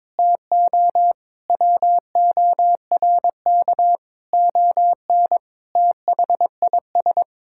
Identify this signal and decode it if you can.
Morse code, heard as a keyed tone, about 22 wpm, 700 Hz. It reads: TO WORK ON THIS
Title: Sentences from Top 1000 Words - Review Part 6 - 22wpm